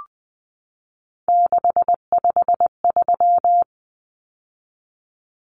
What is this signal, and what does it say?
Morse code, heard as a keyed tone, about 20 wpm, 700 Hz. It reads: 653
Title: Numbers - 3-Digit 20wpm